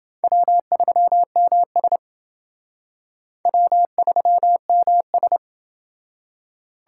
Morse code, heard as a keyed tone, about 28 wpm, 700 Hz. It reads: W3MS W3MS